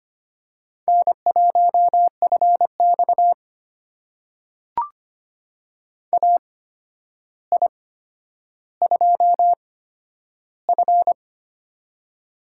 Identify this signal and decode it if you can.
Morse code, heard as a keyed tone, about 25 wpm, spaced as for 11 wpm, 700 Hz. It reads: N1FX AI2F
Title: USA Call Signs - 4 Characters - 8x Character Spacing - 25wpm